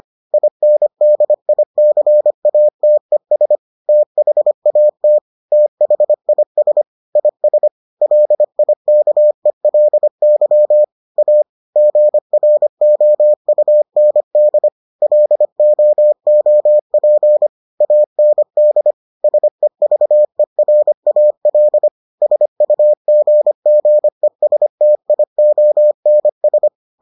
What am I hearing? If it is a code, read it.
Morse code, heard as a keyed tone, about 25 wpm, 600 Hz. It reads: INDICATES THAT THIS IS LIKELY A GROUND LOOP AND SEVERAL SUGGESTIONS